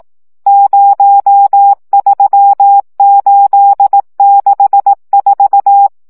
Morse code, heard as a keyed tone, about 18 wpm, 800 Hz. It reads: 03864
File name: morze vizsga